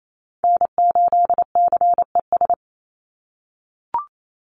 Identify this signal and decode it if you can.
Morse code, heard as a keyed tone, about 28 wpm, 700 Hz. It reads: N8CES
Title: USA Call Signs 28wpm